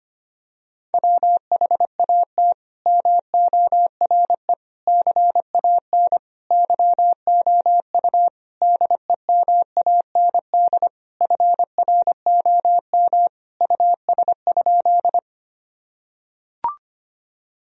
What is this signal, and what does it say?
Morse code, heard as a keyed tone, about 25 wpm, 700 Hz. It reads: WHAT MORE CAN YOU DEMAND FROM US?
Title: Sentences from Top 1000 Words - Review Part 4 - 25wpm